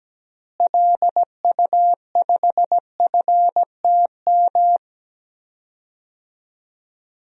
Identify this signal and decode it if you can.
Morse code, heard as a keyed tone, about 17 wpm, 700 Hz. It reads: LU5FTM
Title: DXCC-100 Call Signs - 6 Characters - Repeated 2x - 17wpm